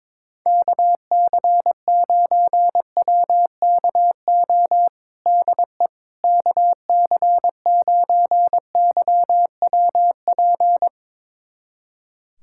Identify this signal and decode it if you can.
Morse code, heard as a keyed tone, about 21 wpm, 700 Hz. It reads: KC9WKO DE KC9YWP
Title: USA Callsign DE Callsign 22wpm